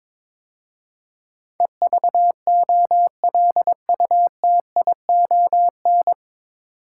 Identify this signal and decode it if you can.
Morse code, heard as a keyed tone, about 22 wpm, 700 Hz. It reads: EVOLUTION
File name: 9 Letter Words - First 3 Letters Given - 22wpm